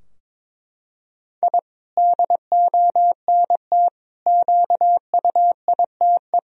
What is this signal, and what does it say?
Morse code, heard as a keyed tone, about 22 wpm, 700 Hz. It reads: I DONT QUITE
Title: Sentences from Top 1000 Words - Review Part 4 - 22wpm